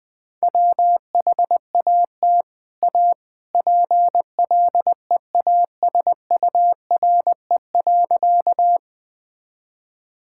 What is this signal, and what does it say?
Morse code, heard as a keyed tone, about 21 wpm, 700 Hz. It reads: WHAT A PLEASURE.
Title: Sentences from Top 1600 Words 20wpm